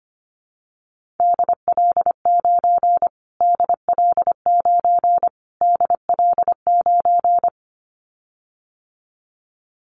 Morse code, heard as a keyed tone, about 25 wpm, 700 Hz. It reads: DL9 DL9 DL9
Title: Callsign Prefixes for Top 25 DXCC - Repeated 3x - 25wpm